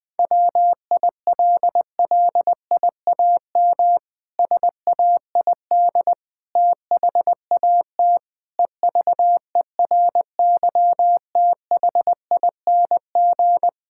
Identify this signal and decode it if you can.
Morse code, heard as a keyed tone, about 20 wpm, 700 Hz. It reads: WILLIAM SAID THAT EVERYTHING
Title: Pangram Challenge v2 - 20wpm